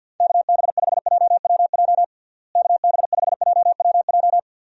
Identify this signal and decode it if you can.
Morse code, heard as a keyed tone, about 47 wpm, 700 Hz. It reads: KB5JWJ KB5JWJ
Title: USA Call Signs - 6 Characters - Repeated 2x - 50wpm